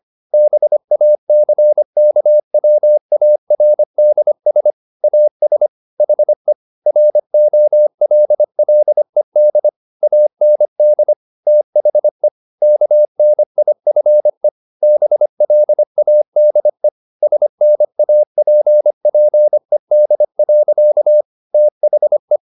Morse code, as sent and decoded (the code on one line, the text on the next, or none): -... .- -.-. -.- .-- .- .-. -.. ... / .- ... / .... / .-. --- .-.. .-.. -.. / .- -. -.. / - .... / -.- -. .. ..-. / -... .-.. .- -.. / ... -. .- .--. .--. -.. .-.-.- / - .... .
BACKWARDS AS HE ROLLED AND THE KNIFE BLADE SNAPPED. THE